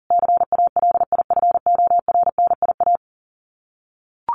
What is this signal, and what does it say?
Morse code, heard as a keyed tone, about 40 wpm, 700 Hz. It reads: CALIFORNIA
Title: Sets of 5 Words Encore2 40wpm